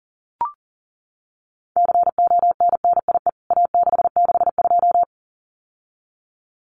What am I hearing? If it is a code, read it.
Morse code, heard as a keyed tone, about 40 wpm, 700 Hz. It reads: CONNIE A662
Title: CWT Exchange 40wpm